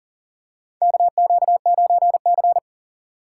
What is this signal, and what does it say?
Morse code, heard as a keyed tone, about 40 wpm, 700 Hz. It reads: KQ9C
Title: USA Call Signs - 4 Characters - 4x Character Spacing - 40wpm